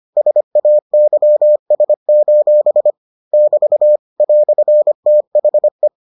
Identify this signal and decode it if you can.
Morse code, heard as a keyed tone, about 25 wpm, 600 Hz. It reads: SAYS: <BT> "THE